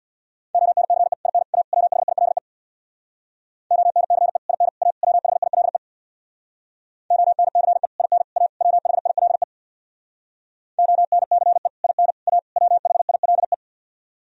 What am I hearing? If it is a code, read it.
Morse code, heard as a keyed tone, about 64 wpm, 700 Hz. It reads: ONCE IN A WHILE ONCE IN A WHILE ONCE IN A WHILE ONCE IN A WHILE